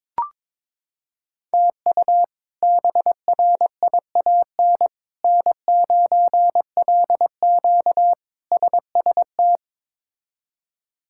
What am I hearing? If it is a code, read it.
Morse code, heard as a keyed tone, about 22 wpm, 700 Hz. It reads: TU BRIAN N9LQ SST
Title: K1USN SST QSO Practice - 22wpm